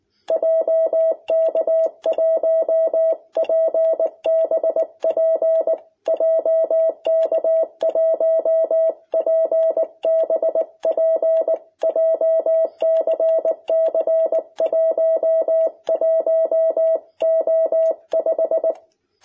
Morse code, heard as a keyed tone, about 19 wpm, 660 Hz. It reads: JK1PBP JK1PBP JCC 11O5
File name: jh8cbh / 2021cw